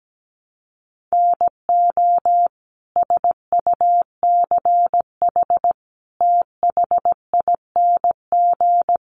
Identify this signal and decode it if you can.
Morse code, heard as a keyed tone, about 17 wpm, 700 Hz. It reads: NO SUCH THING
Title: Sets of 5 Words Encore 17wpm